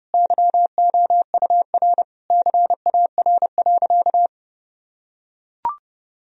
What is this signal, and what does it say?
Morse code, heard as a keyed tone, about 30 wpm, 700 Hz. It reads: YOUR CAR.
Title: Sentences from Top 1000 Words - Review Part 1 - 30wpm